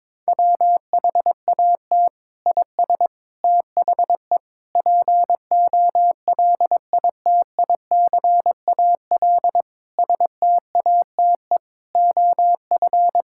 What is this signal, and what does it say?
Morse code, heard as a keyed tone, about 22 wpm, 700 Hz. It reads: WHAT IS THE POLITICAL STATE OF
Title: Sentences from Top 1000 Words - Review Part 5 - 22wpm